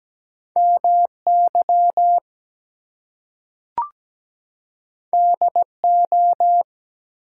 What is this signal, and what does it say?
Morse code, heard as a keyed tone, about 17 wpm, 700 Hz. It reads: MY DO
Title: Two Letter Words - 17wpm